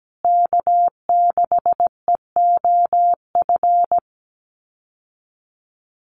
Morse code, heard as a keyed tone, about 17 wpm, 700 Hz. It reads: K6EOF
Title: USA Call Signs Repeated 3x 17wpm